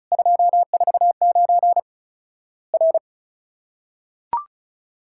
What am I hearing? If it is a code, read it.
Morse code, heard as a keyed tone, about 36 wpm, 700 Hz. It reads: 249 R
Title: Worked All Europe DX Contest - QTC Exchange - 35wpm